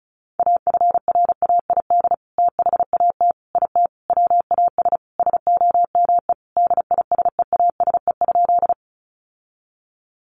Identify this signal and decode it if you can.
Morse code, heard as a keyed tone, about 34 wpm, 700 Hz. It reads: AFRAID THAT IT WAS SOME DISEASE?
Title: Sentences from Top 1500 Words 35wpm